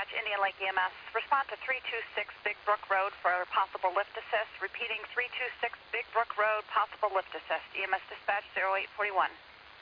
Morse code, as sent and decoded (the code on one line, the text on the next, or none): none